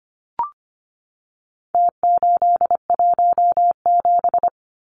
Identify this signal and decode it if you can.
Morse code, heard as a keyed tone, about 25 wpm, 700 Hz. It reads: T817
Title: Numbers - Contest Style - 4-Digit 25wpm